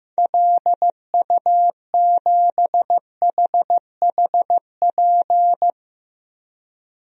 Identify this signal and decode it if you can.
Morse code, heard as a keyed tone, about 15 wpm, 700 Hz. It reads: LU7HHP
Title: DXCC-100 Call Signs - 6 Characters - Repeated 2x - 15wpm